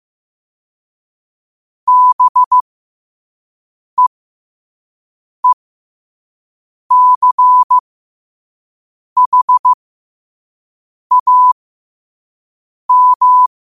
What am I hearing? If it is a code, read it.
Morse code, heard as a keyed tone, about 15 wpm, spaced as for 5 wpm, 1000 Hz. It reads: BEECHAM
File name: qotdfiles / 5wpm